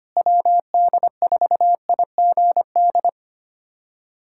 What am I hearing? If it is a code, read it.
Morse code, heard as a keyed tone, about 25 wpm, 700 Hz. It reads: WD4IGD
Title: USA Callsign DE Callsign 25wpm